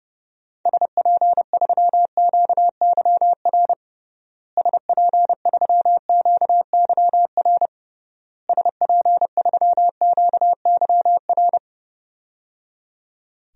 Morse code, heard as a keyed tone, about 30 wpm, 700 Hz. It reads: SP3QYR SP3QYR SP3QYR
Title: DXCC-100 Call Signs - 6 Characters - Repeated 3x - 30wpm